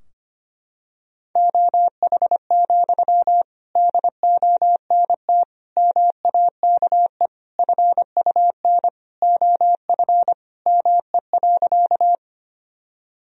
Morse code, heard as a keyed tone, about 25 wpm, 700 Hz. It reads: OH, DONT MAKE FUN OF ME.
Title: Sentences from Top 1000 Words - Review Part 4 - 25wpm